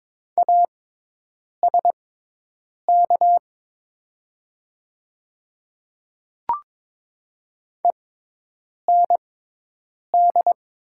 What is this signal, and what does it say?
Morse code, heard as a keyed tone, about 22 wpm, spaced as for 8 wpm, 700 Hz. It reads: ASK END
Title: 6x Character Spacing - Top 200 Words - 22wpm